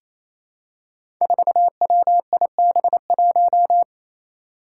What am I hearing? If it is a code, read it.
Morse code, heard as a keyed tone, about 28 wpm, 700 Hz. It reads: 4WIB1
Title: Five Challenging Letter-Number Combinations 28wpm